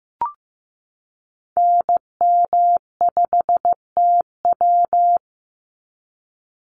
Morse code, heard as a keyed tone, about 15 wpm, 700 Hz. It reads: NM5TW
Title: NA SOTA Activator Callsigns 15wpm